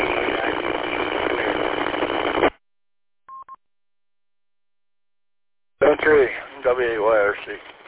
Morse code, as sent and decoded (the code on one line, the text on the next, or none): none